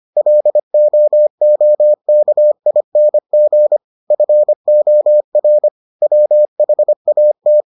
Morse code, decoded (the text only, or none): LOOKING FOR WHAT